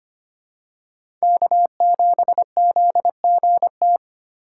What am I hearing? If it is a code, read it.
Morse code, heard as a keyed tone, about 25 wpm, 690 Hz. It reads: K7ZGT
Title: USA Call Signs - 5 Characters - 6x Character Spacing - 25wpm